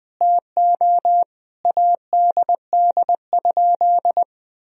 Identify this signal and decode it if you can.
Morse code, heard as a keyed tone, about 19 wpm, 700 Hz. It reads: TO ADD?
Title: Sentences from Top 1000 Words - Review Part 1 - 20wpm